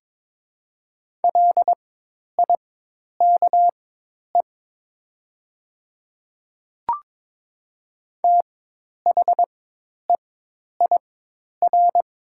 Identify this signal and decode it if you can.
Morse code, heard as a keyed tone, about 22 wpm, spaced as for 10 wpm, 700 Hz. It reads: LIKE THEIR
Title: 4x Character Spacing - Top 300 Words - 22wpm